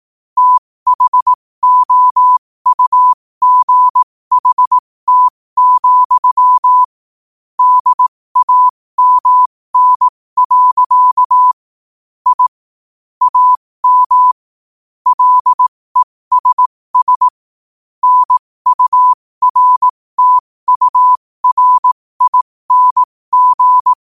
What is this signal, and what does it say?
Morse code, heard as a keyed tone, about 18 wpm, spaced as for 15 wpm, 1000 Hz. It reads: THOUGHT, DAMN. I AM LESS NURTURING
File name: qotdfiles / 15wpm